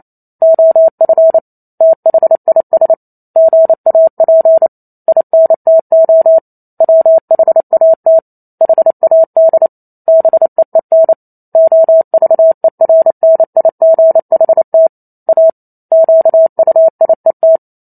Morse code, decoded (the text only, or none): OF THIS GAP INTO WHAT HAD BEEN OVERNIGHT A QUIET